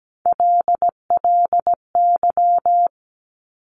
Morse code, heard as a keyed tone, about 17 wpm, 700 Hz. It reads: LLY
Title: Top Three Letter Combinations in Words 17wpm